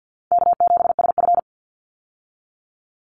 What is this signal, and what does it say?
Morse code, heard as a keyed tone, about 50 wpm, 700 Hz. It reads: K7SF
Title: Easy Call Signs Repeated 2x - 50wpm